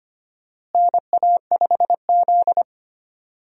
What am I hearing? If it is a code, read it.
Morse code, heard as a keyed tone, about 25 wpm, 700 Hz. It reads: NA5Z
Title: USA Call Signs - 4 Characters - Repeated 2x - 25wpm